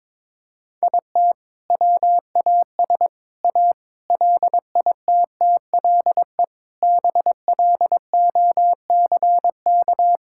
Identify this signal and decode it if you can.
Morse code, heard as a keyed tone, about 22 wpm, 700 Hz. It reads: IT WAS A LITTLE BLOCK
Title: Sentences from Top 1600 Words 22wpm